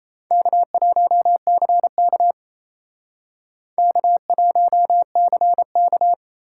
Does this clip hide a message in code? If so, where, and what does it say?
Yes: Morse code from 0.3 to 6.1 s, heard as a keyed tone, about 37 wpm, 700 Hz. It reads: K1CK K1CK